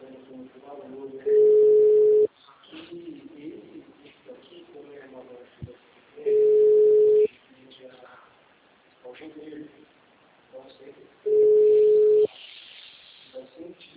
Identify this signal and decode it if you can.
Ringback tone (Europe, 425 Hz), with speech.
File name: out- -2003- - - .22735